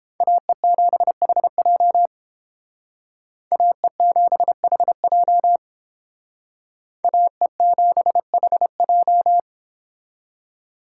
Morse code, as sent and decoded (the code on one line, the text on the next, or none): .- . --... .... .--- / .- . --... .... .--- / .- . --... .... .---
AE7HJ AE7HJ AE7HJ